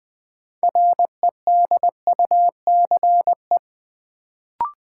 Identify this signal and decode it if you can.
Morse code, heard as a keyed tone, about 20 wpm, 700 Hz. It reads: REDUCE
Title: Sentences from Top 1100 Words - New Word Drill Alt - 20wpm